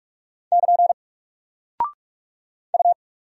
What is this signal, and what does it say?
Morse code, heard as a keyed tone, about 44 wpm, 700 Hz. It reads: <KN> U
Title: Single Character (Letter:Number:Punctuation:Prosign) Alt 45wpm